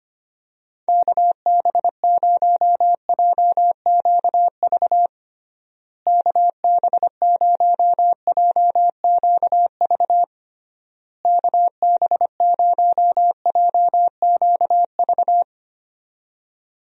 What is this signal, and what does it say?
Morse code, heard as a keyed tone, about 25 wpm, 700 Hz. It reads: KB0JQV KB0JQV KB0JQV
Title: USA Call Signs - 6 Characters - Repeated 3x - 25wpm